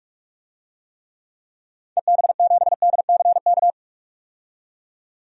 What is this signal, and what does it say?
Morse code, heard as a keyed tone, about 41 wpm, 700 Hz. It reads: EB7DCK
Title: DXCC-100 Call Signs - 6 Characters - 45wpm